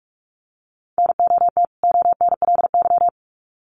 Morse code, heard as a keyed tone, about 44 wpm, 700 Hz. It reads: NOT ONLY
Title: Sets of 2 Words 45wpm